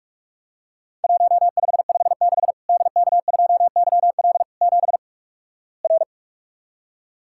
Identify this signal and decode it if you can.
Morse code, heard as a keyed tone, about 48 wpm, 700 Hz. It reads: 1556 DK2YL 7 R
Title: Worked All Europe DX Contest - QTC Exchange - 45wpm